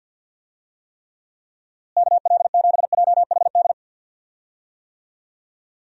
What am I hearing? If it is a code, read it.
Morse code, heard as a keyed tone, about 50 wpm, 700 Hz. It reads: KL7JHD
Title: USA Call Signs - 6 Characters - Repeated 3x - 50wpm